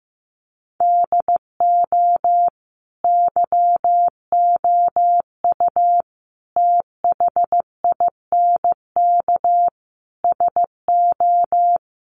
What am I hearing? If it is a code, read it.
Morse code, heard as a keyed tone, about 15 wpm, 700 Hz. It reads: DO YOU THINK SO
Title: Sentences from Top 200 Words 15wpm